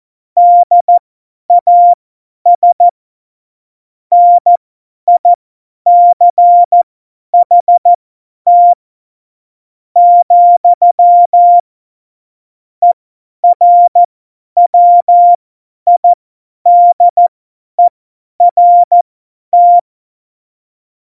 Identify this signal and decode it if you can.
Morse code, heard as a keyed tone, about 14 wpm, spaced as for 10 wpm, 700 Hz. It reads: DAS NICHT , ERWIDERT
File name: Morsekurs / Witze / Witz 18. Dezember, 50 F70